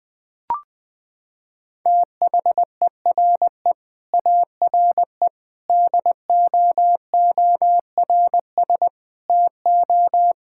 Morse code, heard as a keyed tone, about 20 wpm, 700 Hz. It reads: THERE ARE DOORS TO
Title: Sentences with 8-Word Phrases - 20wpm